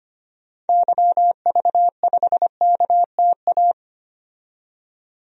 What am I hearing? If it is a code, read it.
Morse code, heard as a keyed tone, about 25 wpm, 700 Hz. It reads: YV5KTA
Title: DXCC-100 Call Signs - 6 Characters - 25wpm